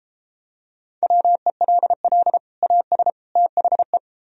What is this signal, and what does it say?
Morse code, heard as a keyed tone, about 32 wpm, 700 Hz. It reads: WELL AS THE